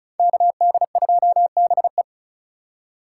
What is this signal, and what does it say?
Morse code, heard as a keyed tone, about 36 wpm, 700 Hz. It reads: KD2BE
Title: USA Call Signs - 5 Characters - 2x Character Spacing - 35wpm